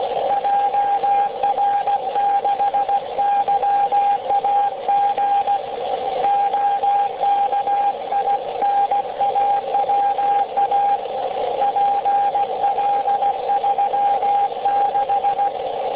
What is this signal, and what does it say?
Morse code, heard as a keyed tone, about 16 wpm, 800 Hz, with radio noise. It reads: JR6YAG OKINAWA PL36